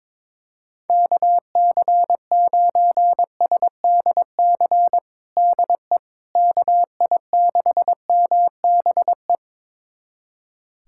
Morse code, heard as a keyed tone, about 21 wpm, 700 Hz. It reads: KC9SDC DE KI6MBE
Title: USA Callsign DE Callsign - 6 Characters - 22wpm